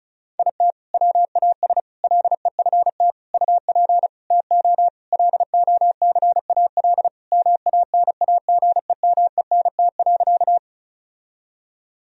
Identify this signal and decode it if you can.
Morse code, heard as a keyed tone, about 34 wpm, 700 Hz. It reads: IT WAS LEFT UP TO LOCAL MANAGEMENT.
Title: Sentences from Top 1000 Words - Review Part 4 - 35wpm